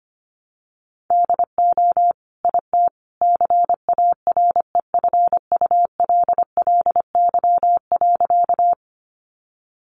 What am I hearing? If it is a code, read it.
Morse code, heard as a keyed tone, about 25 wpm, 700 Hz. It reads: DO IT CAREFULLY.
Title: Sentences from Top 1300 Words 25wpm